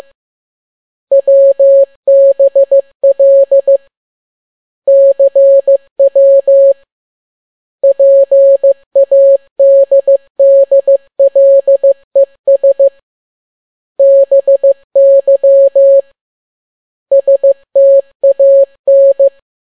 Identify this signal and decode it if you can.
Morse code, heard as a keyed tone, about 15 wpm, 550 Hz. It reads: WBL CW PADDLES BY STAN